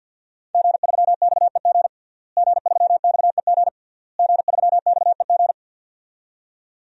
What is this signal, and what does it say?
Morse code, heard as a keyed tone, about 52 wpm, 700 Hz. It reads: G3XEG G3XEG G3XEG
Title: DXCC-100 Call Signs - 5 Characters - Repeated 3x - 50wpm